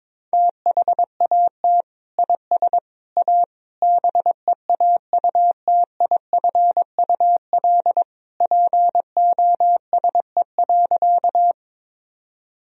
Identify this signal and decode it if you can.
Morse code, heard as a keyed tone, about 22 wpm, 700 Hz. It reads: THAT IS A BEAUTIFUL POSE.
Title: Sentences from Top 1000 Words - Review Part 2 - 22wpm